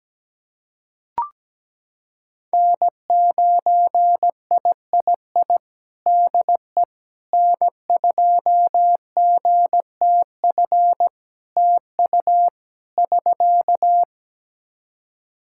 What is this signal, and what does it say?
Morse code, heard as a keyed tone, about 17 wpm, 700 Hz. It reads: N9III DE N2GTF TU <SK>
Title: Practice QSO - Exchange 4 - Ending - 17wpm